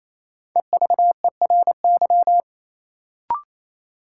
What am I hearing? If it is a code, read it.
Morse code, heard as a keyed tone, about 28 wpm, 700 Hz. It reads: EVERY
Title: Sets of 5 Words - New Word Drill - 28wpm